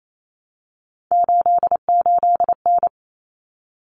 Morse code, heard as a keyed tone, about 28 wpm, 700 Hz. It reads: 88N
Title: Common Cut Numbers - 3-Digit 28wpm